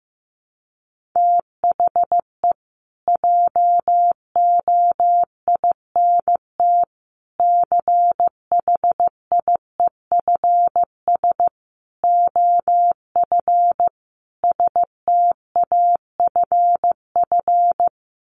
Morse code, heard as a keyed tone, about 15 wpm, 700 Hz. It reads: THE JOINT CHIEFS OF STAFF
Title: Sets of 5 Words Encore2 15wpm